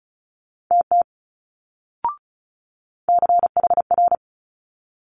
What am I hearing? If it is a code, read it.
Morse code, heard as a keyed tone, about 36 wpm, 700 Hz. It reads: TT CHR
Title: Multi-letter Phonemes Quick Fire 35wpm